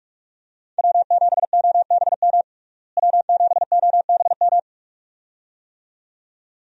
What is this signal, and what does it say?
Morse code, heard as a keyed tone, about 42 wpm, 700 Hz. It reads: W7OBM W7OBM
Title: USA Call Signs - 5 Characters - Repeated 2x - 45wpm